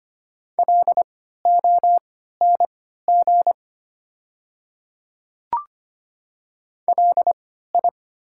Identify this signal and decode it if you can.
Morse code, heard as a keyed tone, about 25 wpm, spaced as for 14 wpm, 690 Hz. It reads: LONG LI
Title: 3x Character Spacing - Top 100 Words - 25wpm